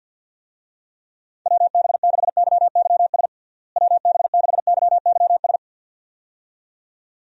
Morse code, heard as a keyed tone, about 47 wpm, 700 Hz. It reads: WB6YYS WB6YYS
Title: USA Call Signs - 6 Characters - Repeated 2x - 50wpm